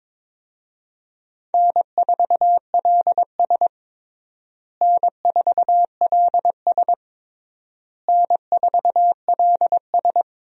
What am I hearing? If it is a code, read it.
Morse code, heard as a keyed tone, about 22 wpm, 700 Hz. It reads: N4LS N4LS N4LS
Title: USA Call Signs - 4 Characters - Repeated 3x - 22wpm